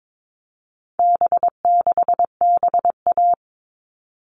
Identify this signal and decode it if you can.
Morse code, heard as a keyed tone, about 22 wpm, 700 Hz. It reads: B6BA
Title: All Cut Numbers - 4-Digit 22wpm